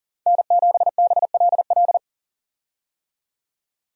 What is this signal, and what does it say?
Morse code, heard as a keyed tone, about 40 wpm, 700 Hz. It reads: N7BLL
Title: USA Call Signs - 5 Characters - 40wpm